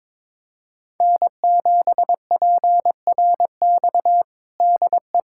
Morse code, heard as a keyed tone, about 22 wpm, 700 Hz. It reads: N7PRX DE
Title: USA Callsign DE Callsign - 5 Characters - 22wpm